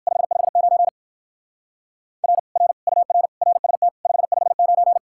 Morse code, heard as a keyed tone, about 59 wpm, 700 Hz. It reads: 559 R R UR RST 559